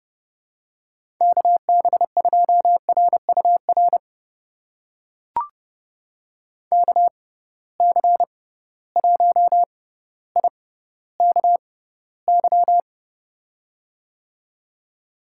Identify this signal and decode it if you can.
Morse code, heard as a keyed tone, about 31 wpm, spaced as for 15 wpm, 700 Hz. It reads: KB2RUR KC1IKY
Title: USA Call Signs - 6 Characters - 6x Character Spacing - 30wpm